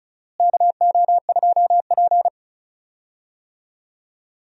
Morse code, heard as a keyed tone, about 36 wpm, 700 Hz. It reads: KO2P
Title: USA Call Signs - 4 Characters - Repeated 2x - 35wpm